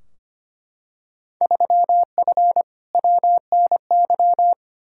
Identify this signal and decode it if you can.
Morse code, heard as a keyed tone, about 25 wpm, 700 Hz. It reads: 3F WNY
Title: Field Day Exchanges - 25wpm